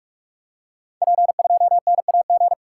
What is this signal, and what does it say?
Morse code, heard as a keyed tone, about 44 wpm, 700 Hz. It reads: P2NAG
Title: Code Groups - 5 Letter:Numbers - 45wpm